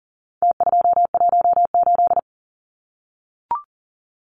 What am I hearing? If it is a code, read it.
Morse code, heard as a keyed tone, about 40 wpm, 700 Hz. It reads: T218
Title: Numbers - Contest Style - 4-Digit 40wpm